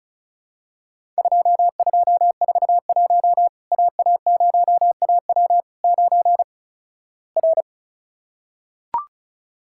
Morse code, heard as a keyed tone, about 36 wpm, 700 Hz. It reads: 2241 AA0AW 9 R
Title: Worked All Europe DX Contest - QTC Exchange - 35wpm